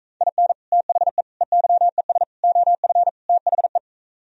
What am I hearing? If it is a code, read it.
Morse code, heard as a keyed tone, about 45 wpm, 700 Hz. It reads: IN THE EYES OF THE